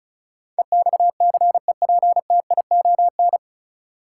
Morse code, heard as a keyed tone, about 36 wpm, 700 Hz. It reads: EXCEPTION
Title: Sets of 4 Words - New Word Drill - 35wpm